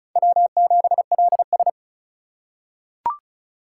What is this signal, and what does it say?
Morse code, heard as a keyed tone, about 36 wpm, 700 Hz. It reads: W7LS
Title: USA Call Signs - 4 Characters - 4x Character Spacing - 35wpm